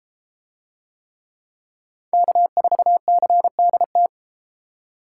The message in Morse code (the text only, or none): K4CDT